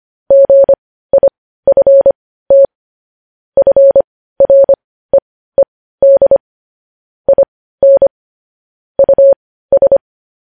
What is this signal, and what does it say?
Morse code, heard as a keyed tone, about 25 wpm, spaced as for 15 wpm, 550 Hz. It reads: GIFT FREED IN US
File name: podcast / News headlines-1x-25x15